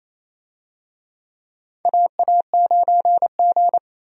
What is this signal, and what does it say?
Morse code, heard as a keyed tone, about 28 wpm, 700 Hz. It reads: AA9G
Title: USA Call Signs - 4 Characters - 28wpm